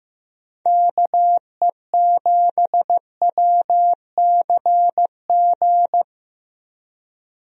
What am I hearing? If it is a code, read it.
Morse code, heard as a keyed tone, about 15 wpm, 700 Hz. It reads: KE7WCG
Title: USA Call Signs - 6 Characters - Repeated 3x - 15wpm